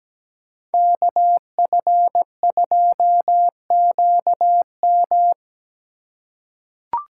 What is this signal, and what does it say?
Morse code, heard as a keyed tone, about 17 wpm, 700 Hz. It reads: KF2QM
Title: USA Call Signs - 3x Character Spacing - 17wpm